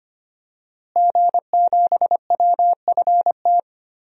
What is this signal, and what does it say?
Morse code, heard as a keyed tone, about 25 wpm, 700 Hz. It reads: G7WFT
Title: DXCC-100 Call Signs - 5 Characters - Repeated 2x - 25wpm